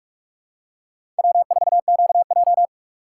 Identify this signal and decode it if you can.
Morse code, heard as a keyed tone, about 42 wpm, 700 Hz. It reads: W4QJ
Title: USA Call Signs - 3x Character Spacing - 45wpm